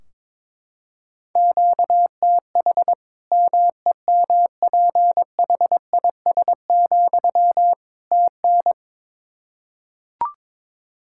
Morse code, heard as a keyed tone, about 22 wpm, 700 Hz. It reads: QTH MEMPHIS, TN
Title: Practice QSO - Exchange 1 - Beginning - 22wpm